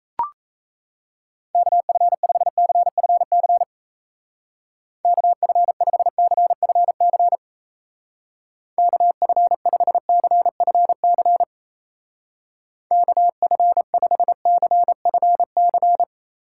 Morse code, heard as a keyed tone, about 36 wpm, 700 Hz. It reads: KF5CFC KF5CFC KF5CFC KF5CFC